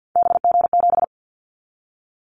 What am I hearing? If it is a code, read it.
Morse code, heard as a keyed tone, about 50 wpm, 700 Hz. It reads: BG7
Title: Callsign Prefixes for Top 100 DXCC - 50wpm